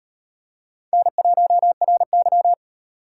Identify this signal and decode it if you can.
Morse code, heard as a keyed tone, about 37 wpm, 700 Hz. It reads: N1RY